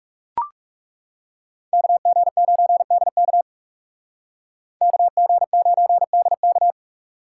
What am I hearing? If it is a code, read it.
Morse code, heard as a keyed tone, about 35 wpm, 700 Hz. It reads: KG9DK KG9DK
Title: Speed Racing - USA 2x2 Call Signs 30wpm